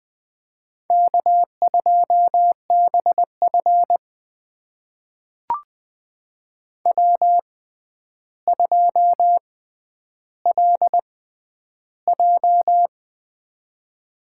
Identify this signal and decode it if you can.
Morse code, heard as a keyed tone, about 20 wpm, spaced as for 10 wpm, 700 Hz. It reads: K2BF W2LJ
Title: USA Call Signs - 4 Characters - 6x Character Spacing - 20wpm